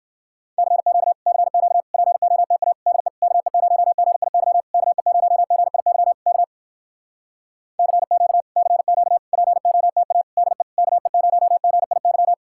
Morse code, heard as a keyed tone, about 59 wpm, 700 Hz. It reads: CQ CQ POTA DE KE0GIY KE0GIY K CQ CQ POTA DE KE0GIY